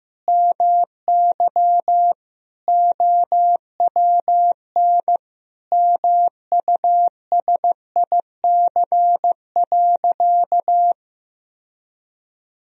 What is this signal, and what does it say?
Morse code, heard as a keyed tone, about 15 wpm, 700 Hz. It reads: MY OWN MUSIC.
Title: Sentences from Top 1000 Words - Review Part 3 - 15wpm